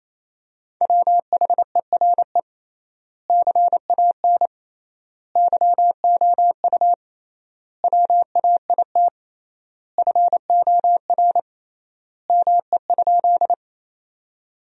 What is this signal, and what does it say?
Morse code, heard as a keyed tone, about 28 wpm, 700 Hz. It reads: WHERE CAN YOU WAIT FOR ME?
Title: 3x Word-Spacing - Sentences from Top 500 Words - 28wpm